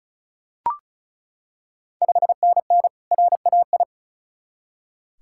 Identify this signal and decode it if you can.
Morse code, heard as a keyed tone, about 36 wpm, 690 Hz. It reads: 5NN RAI
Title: Ontario QSO Party Exchange 35wpm